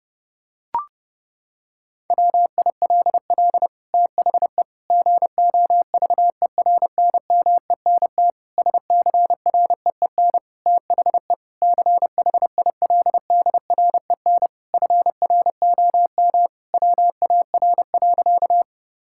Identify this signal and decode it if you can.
Morse code, heard as a keyed tone, about 30 wpm, 700 Hz. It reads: WILL THE GOVERNMENT SCREEN THE CHILDREN FROM WAR.
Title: Sentences from Top 1000 Words - Review Part 6 - 30wpm